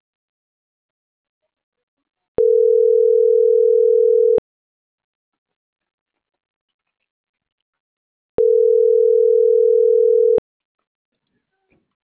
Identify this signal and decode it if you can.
Ringback tone (North America), 440 + 480 Hz.